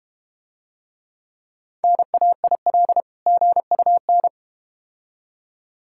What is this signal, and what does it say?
Morse code, heard as a keyed tone, about 31 wpm, 690 Hz. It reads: NAIL GUN